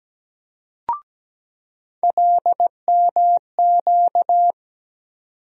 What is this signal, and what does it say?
Morse code, heard as a keyed tone, about 17 wpm, 700 Hz. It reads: LMQ
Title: USA Call Sign Postfixes - 3 Letters 17wpm